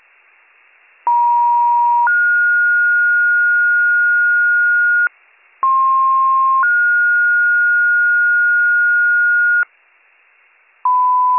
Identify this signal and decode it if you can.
Two-tone page, heard as a one-second tone, then a three-second tone, about 950 then 1430 Hz, about 1060 then 1430 Hz. It is heard from 1.1 to 9.6 s.